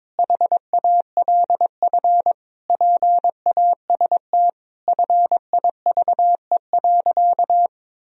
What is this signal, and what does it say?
Morse code, heard as a keyed tone, about 22 wpm, 700 Hz. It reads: HALF PAST FIVE.
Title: Sentences from Top 1000 Words - Review Part 3 - 22wpm